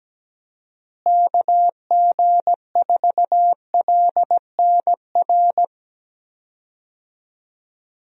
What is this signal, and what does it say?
Morse code, heard as a keyed tone, about 17 wpm, 700 Hz. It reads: KG4LNR
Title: USA Call Signs - 6 Characters - Repeated 3x - 17wpm